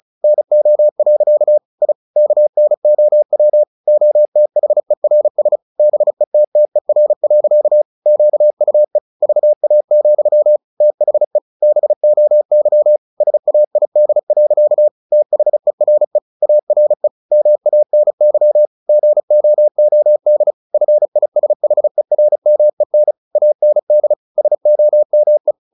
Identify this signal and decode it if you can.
Morse code, heard as a keyed tone, about 34 wpm, 600 Hz. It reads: NO. I KNOW OTHERS BETTER. QUE VA, THE BOY SAID. THERE ARE MANY GOOD FISHERMEN AND SOME
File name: the old man and the sea / the old man and the sea 1 35wpm